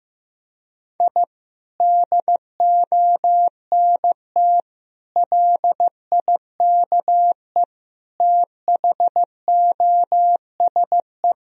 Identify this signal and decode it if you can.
Morse code, heard as a keyed tone, about 16 wpm, 700 Hz. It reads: I DONT LIKE THOSE
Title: Sentences from Top 1000 Words - Review Part 1 - 15wpm